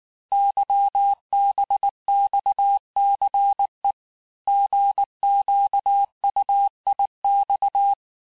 Morse code, heard as a keyed tone, about 19 wpm, 800 Hz. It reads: YBXCE GQUIX
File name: cours-du-lundi files / cours-1595-teste-CHIFFRE-vitesse18-MPM